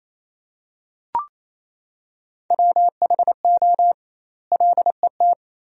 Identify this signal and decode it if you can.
Morse code, heard as a keyed tone, about 28 wpm, 700 Hz. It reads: WHO LET
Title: 2x Word-Spacing - Sentences from Top 300 Words - 28wpm